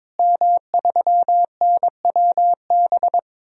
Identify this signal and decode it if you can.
Morse code, heard as a keyed tone, about 22 wpm, 700 Hz. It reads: M3NWB
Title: DXCC-100 Call Signs - 5 Characters - Repeated 3x - 22wpm